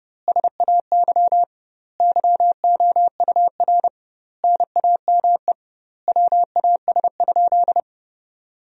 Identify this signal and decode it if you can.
Morse code, heard as a keyed tone, about 32 wpm, 700 Hz. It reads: SAY YOUR NAME WAS?
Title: 2x Word-Spacing - Sentences from Top 200 Words - 30wpm